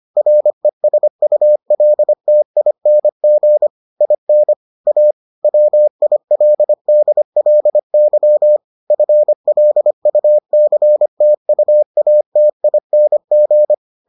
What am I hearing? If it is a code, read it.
Morse code, heard as a keyed tone, about 25 wpm, 600 Hz. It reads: RESULTING IN A WILDLY FLUCTUATING